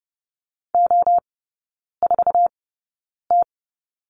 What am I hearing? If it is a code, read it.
Morse code, heard as a keyed tone, about 30 wpm, spaced as for 9 wpm, 690 Hz. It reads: O4T
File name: Lesson 17e - Letter u - Three Characters 30wpm